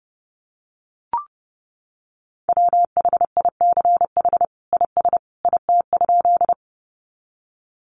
Morse code, heard as a keyed tone, about 30 wpm, 700 Hz. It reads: WHICH IS IT?
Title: Sentences from Top 100 Words 30wpm